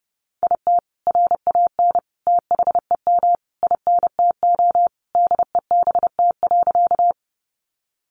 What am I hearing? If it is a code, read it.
Morse code, heard as a keyed tone, about 30 wpm, 700 Hz. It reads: IT RAN THEM INTO DEBT.
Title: Sentences from Top 1300 Words 30wpm